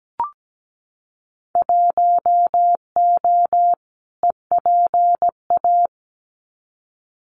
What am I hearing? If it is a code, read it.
Morse code, heard as a keyed tone, about 17 wpm, 700 Hz. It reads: 1O EPA
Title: Winter Field Day Exchanges 17wpm